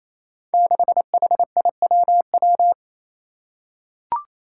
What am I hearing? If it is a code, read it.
Morse code, heard as a keyed tone, about 28 wpm, 700 Hz. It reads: 6HIWW
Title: Five Challenging Letter-Number Combinations 28wpm